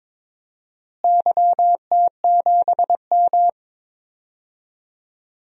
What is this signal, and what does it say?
Morse code, heard as a keyed tone, about 22 wpm, 700 Hz. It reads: YT7M
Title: DXCC-100 Call Signs - 4 Characters - Repeated 3x - 22wpm